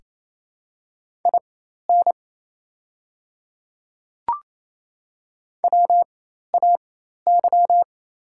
Morse code, heard as a keyed tone, about 28 wpm, spaced as for 13 wpm, 700 Hz. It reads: IN WAY
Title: 4x Character Spacing - Top 100 Words - 28wpm